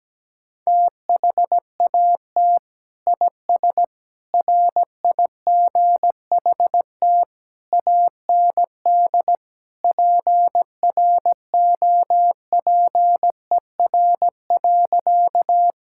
Morse code, heard as a keyed tone, about 17 wpm, 700 Hz. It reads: THAT IS RIGHT AND PROPER.
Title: Sentences from Top 1000 Words - Review Part 5 - 17wpm